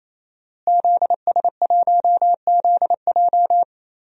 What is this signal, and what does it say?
Morse code, heard as a keyed tone, about 28 wpm, 700 Hz. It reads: ZS1ZJ
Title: DXCC-100 Call Signs - 5 Characters - Repeated 3x - 28wpm